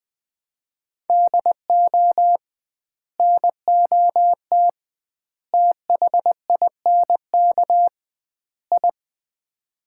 Morse code, heard as a keyed tone, about 19 wpm, 700 Hz. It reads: DO NOT THINK I
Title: 2x Word-Spacing - Sets of 4 Words - 20wpm